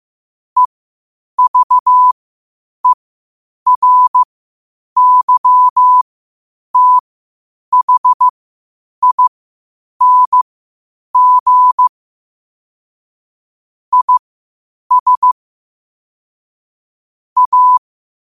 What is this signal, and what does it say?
Morse code, heard as a keyed tone, about 15 wpm, spaced as for 8 wpm, 1000 Hz. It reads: EVERYTHING IS A
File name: qotdfiles / 8wpm